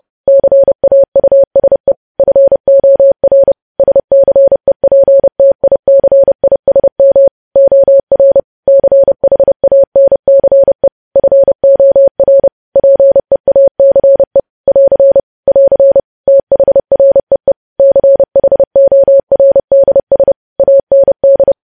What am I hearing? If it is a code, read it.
Morse code, heard as a keyed tone, about 30 wpm, 550 Hz. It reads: CAUSE FOR SCEPTICISM OR CHANCE FOR PEACE <AR> <AR> THREE CHORDS AND
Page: news headlines in morse code at 30 wpm